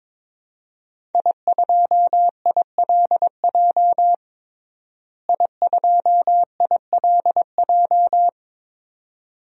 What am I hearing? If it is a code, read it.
Morse code, heard as a keyed tone, about 22 wpm, 700 Hz. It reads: I2ILJ I2ILJ
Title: DXCC-100 Call Signs - 5 Characters - Repeated 2x - 22wpm